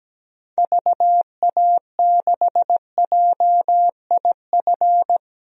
Morse code, heard as a keyed tone, about 17 wpm, 700 Hz. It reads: VA6JIF
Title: DXCC-100 Call Signs - 6 Characters - Repeated 3x - 17wpm